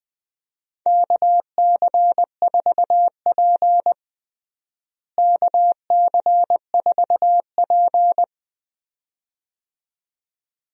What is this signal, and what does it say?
Morse code, heard as a keyed tone, about 20 wpm, 700 Hz. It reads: KC4P KC4P
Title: USA Call Signs - 4 Characters - Repeated 2x - 20wpm